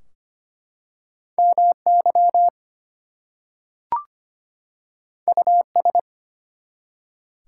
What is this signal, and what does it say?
Morse code, heard as a keyed tone, about 25 wpm, 700 Hz. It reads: MY US
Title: Two Letter Words - 25wpm